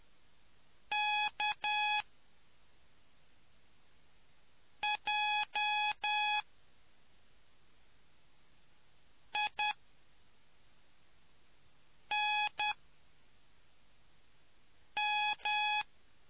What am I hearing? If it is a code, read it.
Morse code, heard as a keyed tone, about 10 wpm, spaced as for 3 wpm, 840 Hz. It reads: KJINM